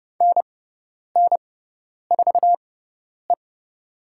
Morse code, heard as a keyed tone, about 31 wpm, spaced as for 10 wpm, 700 Hz. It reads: NN4E
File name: Lesson 16g - Letter c - Call Signs 30wpm